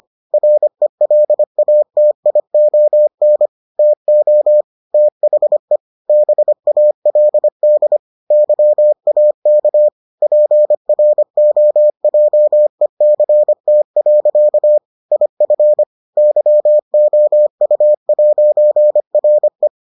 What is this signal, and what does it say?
Morse code, heard as a keyed tone, about 25 wpm, 600 Hz. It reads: RELATION TO THE BALD YAK PROJECT. IF YOU'RE